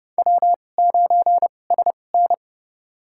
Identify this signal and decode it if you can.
Morse code, heard as a keyed tone, about 29 wpm, spaced as for 21 wpm, 700 Hz. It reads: W9SN
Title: USA Call Signs - 4 Characters - 2x Character Spacing - 30wpm